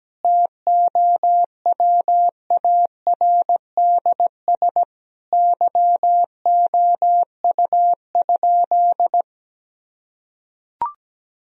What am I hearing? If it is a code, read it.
Morse code, heard as a keyed tone, about 17 wpm, 700 Hz. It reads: TOWARDS YOU?
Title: Sentences from Top 1000 Words - Review Part 2 - 17wpm